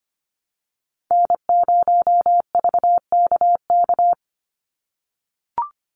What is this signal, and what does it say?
Morse code, heard as a keyed tone, about 25 wpm, 700 Hz. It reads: N0VKK
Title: USA Call Signs 25wpm